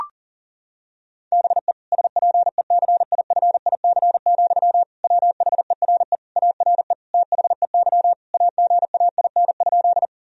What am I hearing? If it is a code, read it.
Morse code, heard as a keyed tone, about 40 wpm, 700 Hz. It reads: BE SPECIFIC, WHERE ARE THEY AGAIN?
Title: Sentences from Top 1000 Words - Review Part 3 - 40wpm